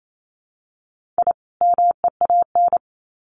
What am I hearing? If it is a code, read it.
Morse code, heard as a keyed tone, about 28 wpm, 700 Hz. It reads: I MEAN
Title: Sets of 2 Words 28wpm